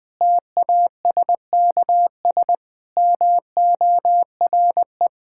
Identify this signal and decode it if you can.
Morse code, heard as a keyed tone, about 20 wpm, 700 Hz. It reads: TASKS MORE
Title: Difficult Sentences with Hints - 20wpm